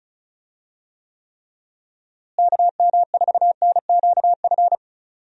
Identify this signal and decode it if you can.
Morse code, heard as a keyed tone, about 34 wpm, 700 Hz. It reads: KM4NQF
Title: USA Call Signs - 6 Characters - Repeated 3x - 35wpm